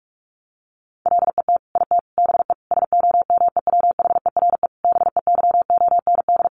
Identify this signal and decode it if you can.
Morse code, heard as a keyed tone, about 44 wpm, 700 Hz. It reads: LET IT BE SOMEWHERE BEYOND
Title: Sentences from Top 1400 Words 45wpm